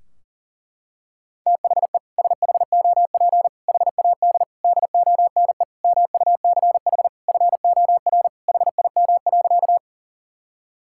Morse code, heard as a keyed tone, about 40 wpm, 700 Hz. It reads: THE SHOP HAD DONE MUCH FOR HIM.
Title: Sentences from Top 1600 Words 40wpm